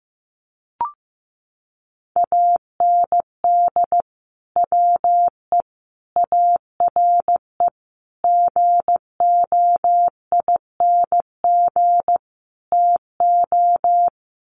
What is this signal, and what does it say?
Morse code, heard as a keyed tone, about 15 wpm, 700 Hz. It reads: AND WE ARE GOING TO
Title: Sets of 5 Words Encore2 15wpm